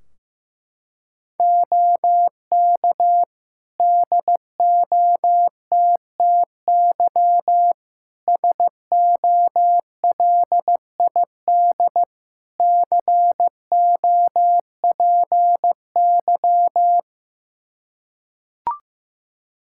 Morse code, heard as a keyed tone, about 16 wpm, 700 Hz. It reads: OK DOTTY SOLID COPY
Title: K1USN SST QSO Practice 15wpm